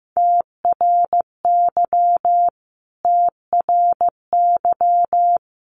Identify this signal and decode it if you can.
Morse code, heard as a keyed tone, about 14 wpm, 700 Hz. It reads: TRY TRY
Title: Top 200 Words Repeated 15wpm